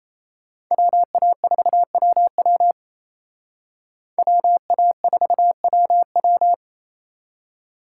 Morse code, heard as a keyed tone, about 36 wpm, 700 Hz. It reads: WA4WW WA4WW